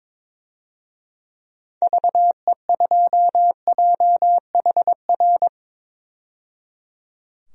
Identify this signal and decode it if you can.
Morse code, heard as a keyed tone, about 22 wpm, 700 Hz. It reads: VE2JHR
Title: DXCC-100 Call Signs - 6 Characters - 22wpm